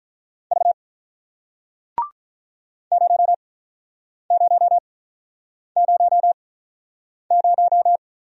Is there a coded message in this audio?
Yes: Morse code from 0.5 to 8.0 s, heard as a keyed tone, about 47 wpm, spaced as for 9 wpm, 700 Hz. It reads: V 0000